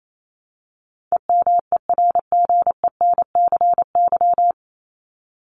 Morse code, heard as a keyed tone, about 28 wpm, 700 Hz. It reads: EMERGENCY
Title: Top 1501-1750 Words 28wpm